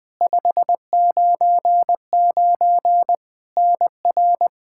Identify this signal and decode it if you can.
Morse code, heard as a keyed tone, about 20 wpm, 700 Hz. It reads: 599 NR
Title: Practice QSO - Exchange 1 - Beginning - 20wpm